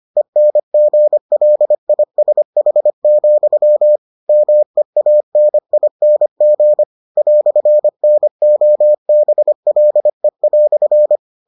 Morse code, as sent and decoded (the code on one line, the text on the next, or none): . -. --. .-.. .. ... .... --..-- / -- . .- -. .. -. --. / .-..-. -. --- -... .-.. .-..-.
ENGLISH, MEANING "NOBLE"